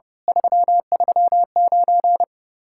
Morse code, heard as a keyed tone, about 37 wpm, 700 Hz. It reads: 339
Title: Speed-Racing Common QSO Phrases - 30wpm